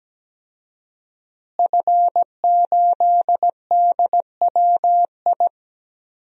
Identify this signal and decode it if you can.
Morse code, heard as a keyed tone, about 17 wpm, 700 Hz. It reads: F8DWI
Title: DXCC-100 Call Signs - 5 Characters - Repeated 3x - 17wpm